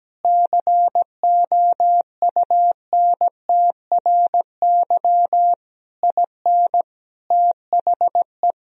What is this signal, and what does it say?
Morse code, heard as a keyed tone, about 17 wpm, 700 Hz. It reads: COUNTRY IN THE
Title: Sentences from Top 1000 Words - Review Part 1 - 17wpm